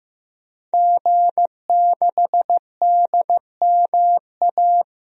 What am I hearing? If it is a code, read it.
Morse code, heard as a keyed tone, about 16 wpm, 700 Hz. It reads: G6DMA
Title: DXCC-100 Call Signs - 5 Characters - Repeated 3x - 15wpm